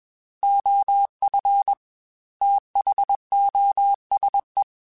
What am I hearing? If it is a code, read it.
Morse code, heard as a keyed tone, about 21 wpm, 800 Hz. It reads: OF THOSE
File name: Technical Subjects / Technical Subjects-mp3s / OR Gate @20